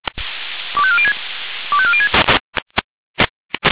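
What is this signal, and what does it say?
Five-tone selective call (ZVEI), heard as quick runs of five tones, tones of about 70 ms. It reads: 25406 25406
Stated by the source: ZVEI five-tone selective call received on an SDR.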